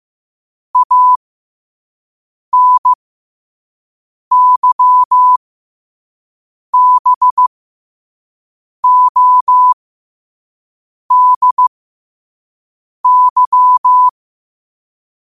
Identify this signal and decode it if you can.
Morse code, heard as a keyed tone, about 15 wpm, spaced as for 5 wpm, 1000 Hz. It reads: ANYBODY
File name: qotdfiles / 5wpm